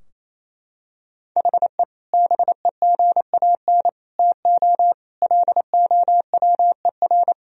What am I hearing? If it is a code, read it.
Morse code, heard as a keyed tone, about 28 wpm, 700 Hz. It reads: HE BEGAN TO LOWER
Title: Sentences from Top 1000 Words - Review Part 4 - 28wpm